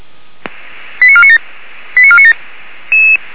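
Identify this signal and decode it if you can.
Five-tone selective call (ZVEI), heard as quick runs of five tones, tones of about 70 ms. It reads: 89497 89497
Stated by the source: ZVEI five-tone selective call 89459 received by SDR.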